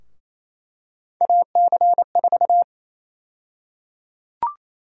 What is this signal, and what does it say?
Morse code, heard as a keyed tone, about 28 wpm, 700 Hz. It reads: AC4
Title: USA Call Sign Prefixes - Repeated 2x - 28wpm